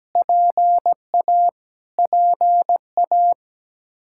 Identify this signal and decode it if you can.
Morse code, heard as a keyed tone, about 17 wpm, 700 Hz. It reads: PA PA
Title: Practice QSO - Exchange 1 - Beginning - 17wpm